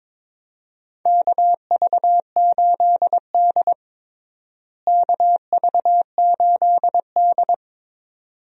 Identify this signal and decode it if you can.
Morse code, heard as a keyed tone, about 22 wpm, 700 Hz. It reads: KV8D KV8D
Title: USA Call Signs - 4 Characters - Repeated 2x - 22wpm